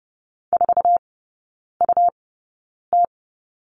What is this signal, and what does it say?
Morse code, heard as a keyed tone, about 30 wpm, spaced as for 9 wpm, 700 Hz. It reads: 4UT
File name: Lesson 17e - Letter u - Three Characters 30wpm